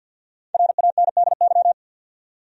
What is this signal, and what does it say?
Morse code, heard as a keyed tone, about 52 wpm, 700 Hz. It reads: RANDY
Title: USA Common Names - Max 5 Letters - 50wpm